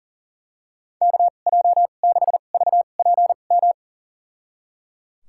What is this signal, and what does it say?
Morse code, heard as a keyed tone, about 37 wpm, spaced as for 28 wpm, 700 Hz. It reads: KJ6VPM
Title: USA Call Signs - 2x Character Spacing - 40wpm